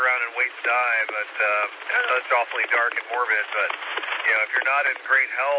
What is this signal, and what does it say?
Emergency Alert System (SAME) tones, heard as harsh buzzing bursts, with speech. It is heard from 0.9 to 4.6 s.